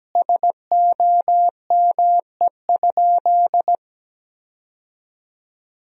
Morse code, heard as a keyed tone, about 17 wpm, 700 Hz. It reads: SOME?
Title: 2x Word-Spacing - Sentences from Top 100 Words - 17wpm